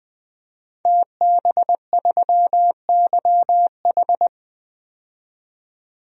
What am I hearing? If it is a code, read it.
Morse code, heard as a keyed tone, about 20 wpm, 700 Hz. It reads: TB3YH
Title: DXCC-100 Call Signs - 5 Characters - 20wpm